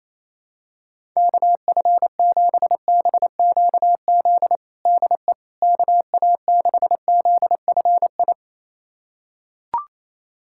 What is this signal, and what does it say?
Morse code, heard as a keyed tone, about 28 wpm, 700 Hz. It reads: KF7BQZ DE KA6ZFI
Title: USA Callsign DE Callsign - 6 Characters - 28wpm